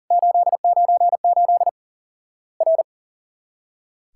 Morse code, heard as a keyed tone, about 40 wpm, 700 Hz. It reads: 898 R
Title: Worked All Europe DX Contest - QTC Exchange - 40wpm